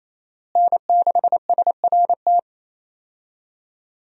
Morse code, heard as a keyed tone, about 27 wpm, 700 Hz. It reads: N6SRT
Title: USA Call Signs - 5 Characters - Repeated 2x - 28wpm